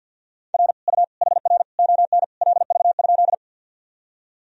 Morse code, heard as a keyed tone, about 52 wpm, 700 Hz. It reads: R U HR ON LV?
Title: Sentences from Top 400 Words and Abbreviations 50wpm